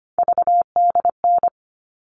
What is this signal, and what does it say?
Morse code, heard as a keyed tone, about 25 wpm, 700 Hz. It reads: VDN
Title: All Cut Numbers - 3-Digit 25wpm